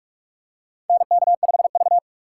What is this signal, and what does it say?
Morse code, heard as a keyed tone, about 43 wpm, 700 Hz. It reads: NK5V
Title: USA Call Signs - 4 Characters - 6x Character Spacing - 45wpm